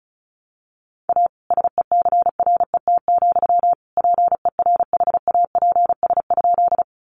Morse code, heard as a keyed tone, about 34 wpm, 700 Hz. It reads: A SECRET, PERHAPS?
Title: Sentences from Top 1200 Words 35wpm